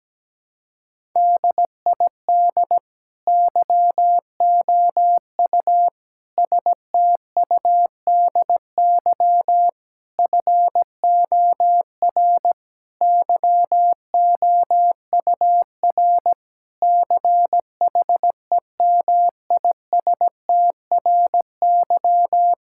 Morse code, heard as a keyed tone, about 17 wpm, 700 Hz. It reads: DID YOU STUDY FOR YOUR CHEMISTRY
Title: Sentences from Top 1000 Words - Review Part 6 - 17wpm